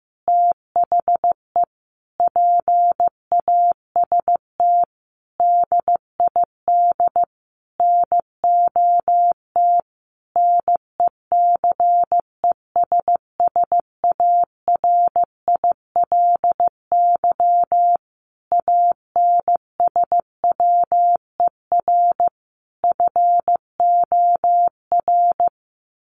Morse code, heard as a keyed tone, about 15 wpm, 700 Hz. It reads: THE PAST DID NOT NECESSARILY ANSWER FOR
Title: Sentences from Top 1400 Words 15wpm